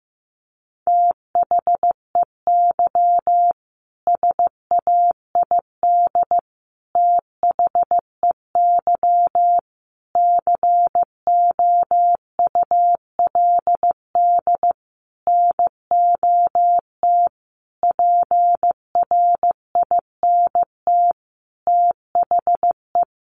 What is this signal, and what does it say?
Morse code, heard as a keyed tone, about 15 wpm, 700 Hz. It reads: THEY SAID THEY COULD NOT PRINT THE
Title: Sentences from Top 1400 Words 15wpm